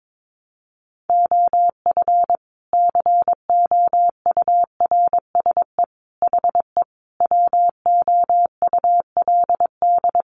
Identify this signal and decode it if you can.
Morse code, heard as a keyed tone, about 22 wpm, 700 Hz. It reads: OF COURSE HE WOULD
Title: Sentences from Top 1200 Words 22wpm